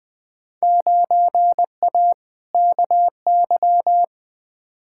Morse code, heard as a keyed tone, about 21 wpm, 700 Hz. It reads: 9A KY
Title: Field Day Exchanges Encore - 20wpm